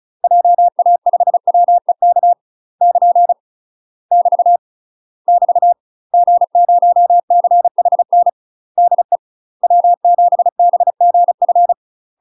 Morse code, heard as a keyed tone, about 35 wpm, 700 Hz. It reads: JA5WEK <KN> <BT> <BT> G0CHN DE W7BGF
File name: UK MASTER QSOs TRAINING / 35WPM / UK MASTER QSOs 0013